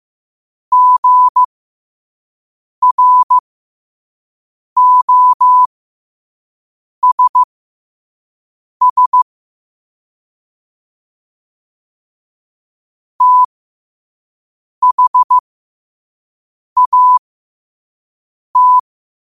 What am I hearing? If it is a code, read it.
Morse code, heard as a keyed tone, about 15 wpm, spaced as for 5 wpm, 1000 Hz. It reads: GROSS THAT
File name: 5wpm